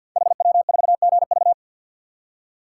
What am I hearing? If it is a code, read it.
Morse code, heard as a keyed tone, about 53 wpm, 700 Hz. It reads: HW4GV
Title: Code Groups - 5 Letter:Numbers - 50wpm